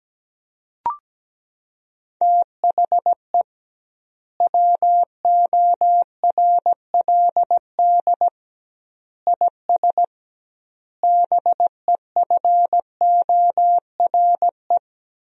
Morse code, heard as a keyed tone, about 17 wpm, 700 Hz. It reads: THE WORLD IS BEFORE
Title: 2x Word-Spacing - Sentences from Top 200 Words - 17wpm